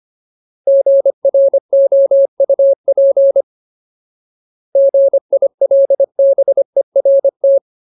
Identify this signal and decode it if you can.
Morse code, heard as a keyed tone, about 25 wpm, 550 Hz. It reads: GROUP GILBERT
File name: Ultimate Word List / 25WPM Ultimate